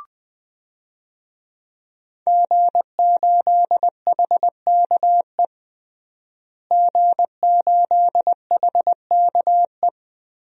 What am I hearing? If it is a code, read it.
Morse code, heard as a keyed tone, about 20 wpm, 700 Hz. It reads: G8HKE G8HKE
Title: DXCC-100 Call Signs - 5 Characters - Repeated 3x - 20wpm